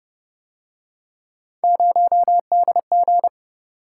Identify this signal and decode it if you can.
Morse code, heard as a keyed tone, about 28 wpm, 700 Hz. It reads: 0DG